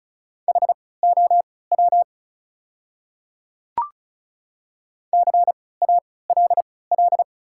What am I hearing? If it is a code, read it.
Morse code, heard as a keyed tone, about 36 wpm, spaced as for 20 wpm, 700 Hz. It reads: HOW CALL
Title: 3x Character Spacing - Top 100 Words - 35wpm